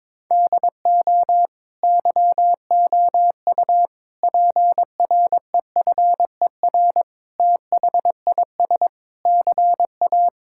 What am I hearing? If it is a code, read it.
Morse code, heard as a keyed tone, about 21 wpm, 700 Hz. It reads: DO YOU PREFER THIS CA
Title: Sentences from Top 1600 Words 22wpm